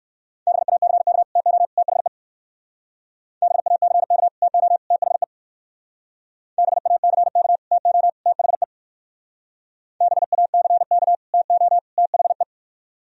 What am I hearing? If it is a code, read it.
Morse code, heard as a keyed tone, about 57 wpm, 700 Hz. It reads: BACK TO THE BACK TO THE BACK TO THE BACK TO THE